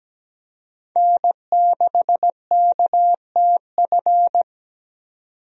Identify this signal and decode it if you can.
Morse code, heard as a keyed tone, about 17 wpm, 700 Hz. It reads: N6KTF
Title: USA Call Signs - 5 Characters - 17wpm